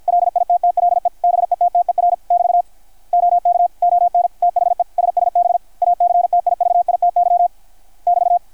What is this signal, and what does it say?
Morse code, heard as a keyed tone, about 53 wpm, 690 Hz, with radio noise. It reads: LITTLE BETTER <BT> OK ON THE SSB ACTIVITY <BT>
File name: archived / CRSnet